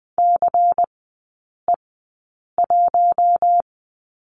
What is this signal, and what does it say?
Morse code, heard as a keyed tone, about 20 wpm, spaced as for 8 wpm, 700 Hz. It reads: CE1
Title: 152 Lesson 25e - Letter p - Three Characters 20wpm